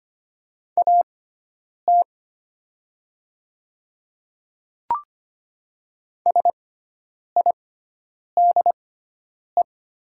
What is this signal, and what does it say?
Morse code, heard as a keyed tone, about 25 wpm, spaced as for 9 wpm, 700 Hz. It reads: AT SIDE
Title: 6x Character Spacing - Top 100 Words - 25wpm